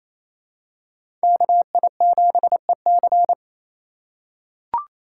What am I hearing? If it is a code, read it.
Morse code, heard as a keyed tone, about 28 wpm, 700 Hz. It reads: KI7EC
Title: USA Call Signs - 6x Character Spacing - 28wpm